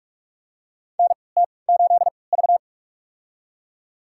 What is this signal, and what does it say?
Morse code, heard as a keyed tone, about 46 wpm, spaced as for 26 wpm, 700 Hz. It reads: NT8V
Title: USA Call Signs - 4 Characters - 3x Character Spacing - 45wpm